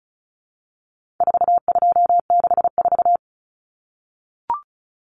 Morse code, heard as a keyed tone, about 34 wpm, 700 Hz. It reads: 4264
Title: Numbers - 4-Digit 35wpm